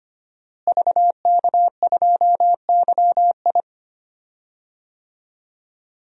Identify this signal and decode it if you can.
Morse code, heard as a keyed tone, about 25 wpm, 700 Hz. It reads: VK2YI
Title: DXCC-100 Call Signs - 5 Characters - Repeated 2x - 25wpm